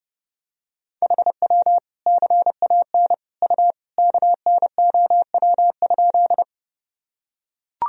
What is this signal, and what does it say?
Morse code, heard as a keyed tone, about 30 wpm, 700 Hz. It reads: HW CAN U KNOW?
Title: Sentences from Top 100 Words and Abbreviations 30wpm